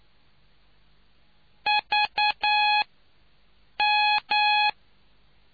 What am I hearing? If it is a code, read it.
Morse code, heard as a keyed tone, about 9 wpm, spaced as for 6 wpm, 840 Hz. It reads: VM